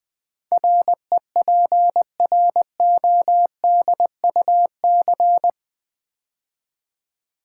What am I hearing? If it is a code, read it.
Morse code, heard as a keyed tone, about 20 wpm, 700 Hz. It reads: REPRODUC
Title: Build A Word - 4 Letters At A Time - 12-Letter Words - 20wpm